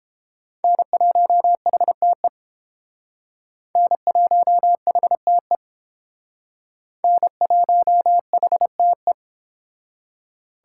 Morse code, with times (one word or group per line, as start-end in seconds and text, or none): 0.6-2.3 N1HTE
3.8-5.6 N1HTE
7.0-9.1 N1HTE